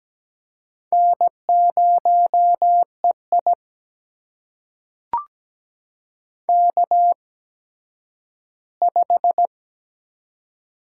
Morse code, heard as a keyed tone, about 17 wpm, spaced as for 10 wpm, 700 Hz. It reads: N0EI K5
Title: USA Call Signs - 4 Characters - 8x Character Spacing - 17wpm